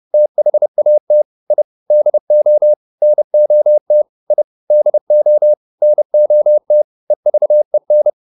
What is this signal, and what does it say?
Morse code, heard as a keyed tone, about 30 wpm, 600 Hz. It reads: THAT I DO NOT I DO NOT EVEN